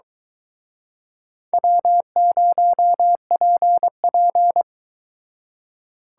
Morse code, heard as a keyed tone, about 23 wpm, 700 Hz. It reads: W0PP